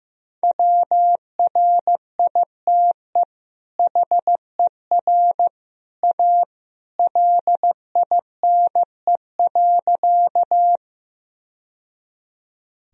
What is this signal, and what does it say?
Morse code, heard as a keyed tone, about 16 wpm, 700 Hz. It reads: WRITE HER A LINE.
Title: Sentences from Top 1000 Words - Review Part 1 - 15wpm